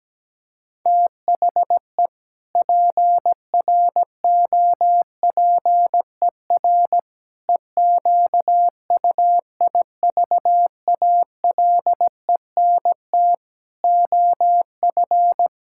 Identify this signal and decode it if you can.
Morse code, heard as a keyed tone, about 17 wpm, 700 Hz. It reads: THE PROPER EQUIVALENT OF
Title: Sentences from Top 1600 Words 17wpm